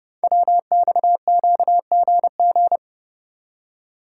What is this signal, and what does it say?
Morse code, heard as a keyed tone, about 29 wpm, 700 Hz. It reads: WXQGG
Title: Code Groups - 5 Letters - 30wpm